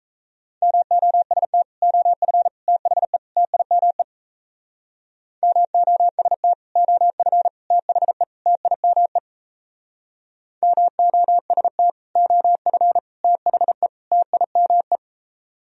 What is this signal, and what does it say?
Morse code, heard as a keyed tone, about 37 wpm, 700 Hz. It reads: MOST OF THE TIME MOST OF THE TIME MOST OF THE TIME